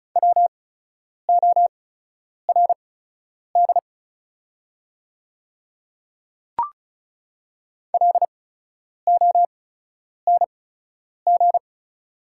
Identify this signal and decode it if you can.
Morse code, heard as a keyed tone, about 36 wpm, spaced as for 10 wpm, 690 Hz. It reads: WORD LONG
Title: 8x Character Spacing - Top 200 Words - 35wpm